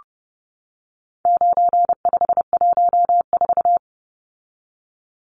Morse code, heard as a keyed tone, about 30 wpm, 700 Hz. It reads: 9514
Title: Numbers - 4-Digit 30wpm